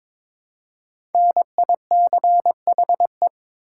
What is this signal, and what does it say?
Morse code, heard as a keyed tone, about 22 wpm, 700 Hz. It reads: NICHE
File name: 5 Letter Words - First 3 Letters Given - 22wpm